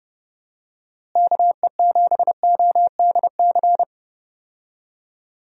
Morse code, heard as a keyed tone, about 30 wpm, 700 Hz. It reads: KE7ODC
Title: USA Call Signs - 6 Characters - Repeated 3x - 30wpm